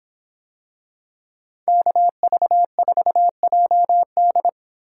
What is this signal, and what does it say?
Morse code, heard as a keyed tone, about 26 wpm, 700 Hz. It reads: KV4JD